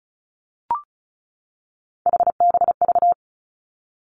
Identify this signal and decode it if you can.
Morse code, heard as a keyed tone, about 36 wpm, 700 Hz. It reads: HBV
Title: Three Challenging Letter-Number Combinations 35wpm